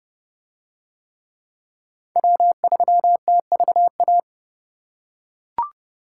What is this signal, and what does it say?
Morse code, heard as a keyed tone, about 28 wpm, 700 Hz. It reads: W3TVA